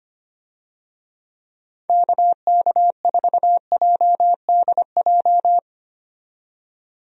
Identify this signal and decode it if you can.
Morse code, heard as a keyed tone, about 25 wpm, 700 Hz. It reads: KK4JDJ
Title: USA Call Signs - 6 Characters - 25wpm